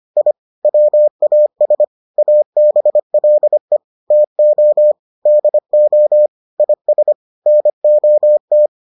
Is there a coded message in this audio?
Yes: Morse code from 0.2 to 8.7 s, heard as a keyed tone, about 25 wpm, 600 Hz. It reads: I WAS ABLE TO DO IS NOT